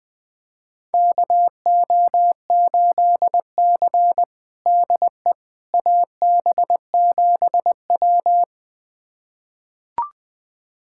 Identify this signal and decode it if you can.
Morse code, heard as a keyed tone, about 20 wpm, 700 Hz. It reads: KO8C DE AB7W
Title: USA Callsign DE Callsign - 4 Characters - 20wpm